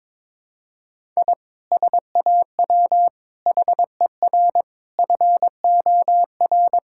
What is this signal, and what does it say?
Morse code, heard as a keyed tone, about 21 wpm, 700 Hz. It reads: I SAW HER FOR
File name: practice / 22wpm / Sentences with 4-Word Phrases - 22wpm